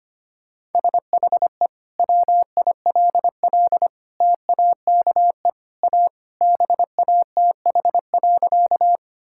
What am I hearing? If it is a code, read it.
Morse code, heard as a keyed tone, about 25 wpm, 700 Hz. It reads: SHE WILL TAKE A BATH.
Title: Sentences from Top 1600 Words 25wpm